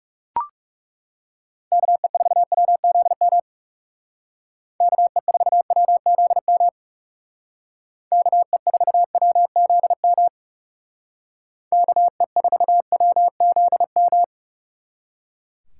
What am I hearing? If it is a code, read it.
Morse code, heard as a keyed tone, about 37 wpm, 690 Hz. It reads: KE4WZM KE4WZM KE4WZM KE4WZM
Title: Speed Racing - USA Call Signs 30wpm